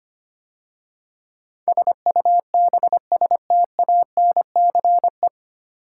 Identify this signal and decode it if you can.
Morse code, heard as a keyed tone, about 25 wpm, 690 Hz. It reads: SUBSTANCE
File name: practice / 25wpm / 9 Letter Words - First 3 Letters Given - 25wpm